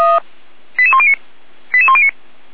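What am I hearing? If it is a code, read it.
Five-tone selective call (ZVEI), heard as quick runs of five tones, tones of about 70 ms. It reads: 88109 88109
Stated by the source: ZVEI five-tone call received by SDR.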